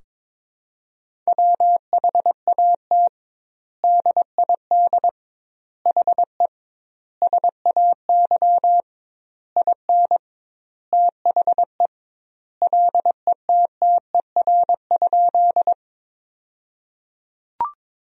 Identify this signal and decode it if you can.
Morse code, heard as a keyed tone, about 22 wpm, 700 Hz. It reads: WHAT DID HE SAY IN THE LETTER?
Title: 2x Word-Spacing - Sentences from Top 300 Words - 22wpm